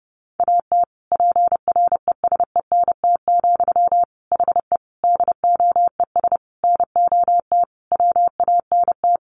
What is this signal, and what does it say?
Morse code, heard as a keyed tone, about 30 wpm, 700 Hz. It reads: AT PRESENT, HE DOES NOT WANT
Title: Sentences from Top 800 Words 30wpm